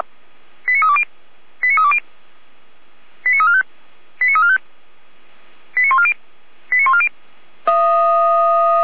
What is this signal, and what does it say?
Five-tone selective call (ZVEI), heard as quick runs of five tones, tones of about 70 ms. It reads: 89320 89320 89345 89345 89140 89140